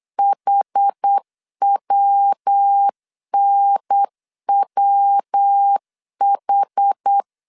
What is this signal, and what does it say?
Morse code, heard as a keyed tone, about 8 wpm, 800 Hz. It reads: HWNWH
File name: CW / DET / 1 3 1500C S B 8wpm 4141